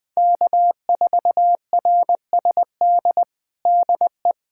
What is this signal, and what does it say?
Morse code, heard as a keyed tone, about 20 wpm, 700 Hz. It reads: K4RSD DE
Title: USA Callsign DE Callsign - 5 Characters - 20wpm